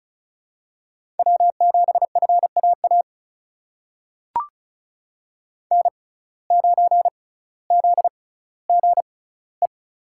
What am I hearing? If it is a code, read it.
Morse code, heard as a keyed tone, about 34 wpm, spaced as for 18 wpm, 700 Hz. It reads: W7FAA N9ZGE
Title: USA Call Signs - 5 Characters - 6x Character Spacing - 35wpm